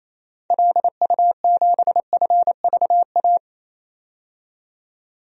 Morse code, heard as a keyed tone, about 28 wpm, 700 Hz. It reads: LU7FVA
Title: DXCC-100 Call Signs - 6 Characters - 28wpm